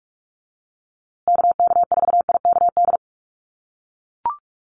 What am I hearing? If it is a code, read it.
Morse code, heard as a keyed tone, about 44 wpm, 700 Hz. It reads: KK4IKD
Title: USA Call Signs Repeated 2x 45wpm